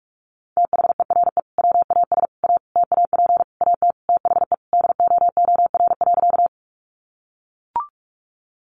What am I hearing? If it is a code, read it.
Morse code, heard as a keyed tone, about 44 wpm, 700 Hz. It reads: THERE WAS A TAP AT THE DOOR.
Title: Sentences from Top 1400 Words 45wpm